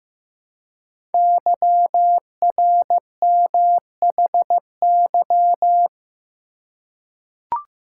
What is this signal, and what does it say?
Morse code, heard as a keyed tone, about 15 wpm, 700 Hz. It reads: YRMHY
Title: Code Groups - 5 Letters - 15wpm